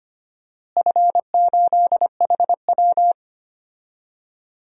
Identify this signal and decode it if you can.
Morse code, heard as a keyed tone, about 25 wpm, 700 Hz. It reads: F8HW
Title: DXCC-100 Call Signs - 4 Characters - Repeated 3x - 25wpm